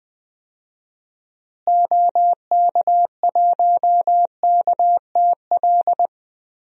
Morse code, heard as a keyed tone, about 20 wpm, 700 Hz. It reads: OK1KTL
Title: DXCC-100 Call Signs - 6 Characters - Repeated 2x - 20wpm